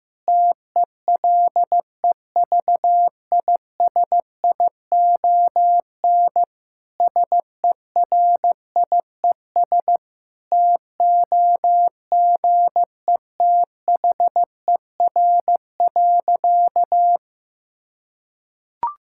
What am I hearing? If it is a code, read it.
Morse code, heard as a keyed tone, about 15 wpm, 700 Hz. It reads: TELEVISION SERIES TOGETHER.
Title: Sentences from Top 1000 Words - Review Part 5 - 15wpm